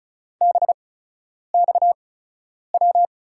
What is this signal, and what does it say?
Morse code, heard as a keyed tone, about 36 wpm, spaced as for 10 wpm, 700 Hz. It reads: BXW
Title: USA Call Signs - 6 Characters - 8x Character Spacing - 35wpm